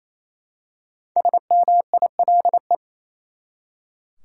Morse code, heard as a keyed tone, about 28 wpm, 700 Hz. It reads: SMILE
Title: Five Letter Words - 28wpm